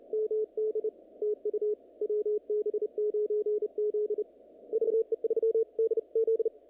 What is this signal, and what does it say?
Morse code, heard as a keyed tone, about 34 wpm, 440 Hz. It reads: MD TU WB9Z VE3DZ
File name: contests / naqp-cw-k5tr-op-k5pi- -2200